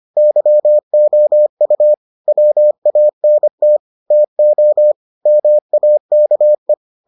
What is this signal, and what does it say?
Morse code, heard as a keyed tone, about 25 wpm, 600 Hz. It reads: YOU WANT TO MAKE